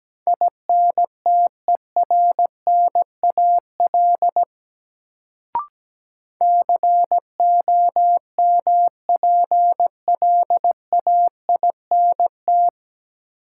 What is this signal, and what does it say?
Morse code, heard as a keyed tone, about 17 wpm, 700 Hz. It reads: INTERNAL COMPLAINT
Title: Sentences from Top 1400 Words - New Word Drill - 17wpm